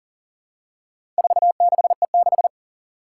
Morse code, heard as a keyed tone, about 40 wpm, 700 Hz. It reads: 46E6
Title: Four Challenging Letter-Number Combinations 40wpm